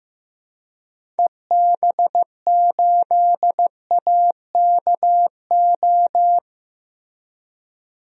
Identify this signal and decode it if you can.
Morse code, heard as a keyed tone, about 14 wpm, 700 Hz. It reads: EB8AKO
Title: DXCC-100 Call Signs - 6 Characters - Repeated 3x - 15wpm